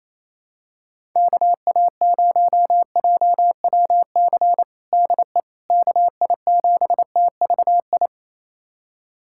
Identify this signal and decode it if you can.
Morse code, heard as a keyed tone, about 28 wpm, 700 Hz. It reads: KA0JWC DE KI7TVI
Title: USA Callsign DE Callsign - 6 Characters - 28wpm